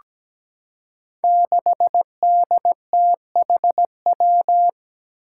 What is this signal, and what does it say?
Morse code, heard as a keyed tone, about 17 wpm, 700 Hz. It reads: 6DTHW
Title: Five Challenging Letter-Number Combinations 17wpm